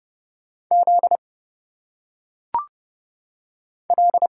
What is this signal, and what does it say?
Morse code, heard as a keyed tone, about 30 wpm, 700 Hz. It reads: Z L
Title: Single Letters - Phonetic Pronunciation 30wpm